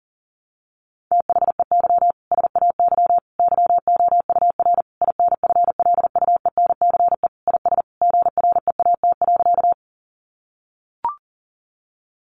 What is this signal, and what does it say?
Morse code, heard as a keyed tone, about 40 wpm, 700 Hz. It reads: THEY SAY YOUR INFLUENCE IS GREAT.
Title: Sentences from Top 1100 Words 40wpm